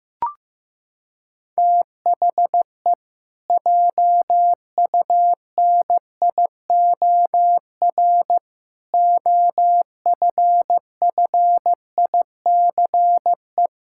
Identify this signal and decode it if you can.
Morse code, heard as a keyed tone, about 15 wpm, 700 Hz. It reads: THE JUNIOR OFFICE
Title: Pangram Challenge v2 - 15wpm